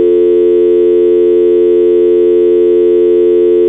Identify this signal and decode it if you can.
Dial tone (North America), 350 + 440 Hz.